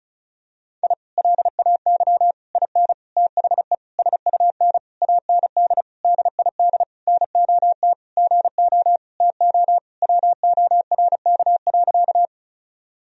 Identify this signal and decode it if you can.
Morse code, heard as a keyed tone, about 36 wpm, 700 Hz. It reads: I LAY IN THE SUN AND DID NOT GO TO WORK.
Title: Sentences from Top 1000 Words - Review Part 2 - 35wpm